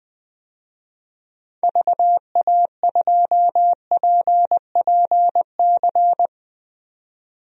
Morse code, heard as a keyed tone, about 20 wpm, 700 Hz. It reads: VA2PPC
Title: DXCC-100 Call Signs - 6 Characters - 20wpm